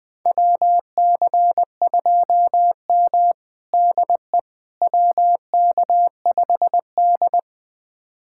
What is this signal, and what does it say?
Morse code, heard as a keyed tone, about 20 wpm, 700 Hz. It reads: WC2M DE WK5D
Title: USA Callsign DE Callsign - 4 Characters - 20wpm